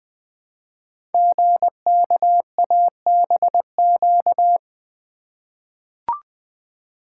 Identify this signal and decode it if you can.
Morse code, heard as a keyed tone, about 20 wpm, 700 Hz. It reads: GKABQ
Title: Code Groups - 5 Letters - 20wpm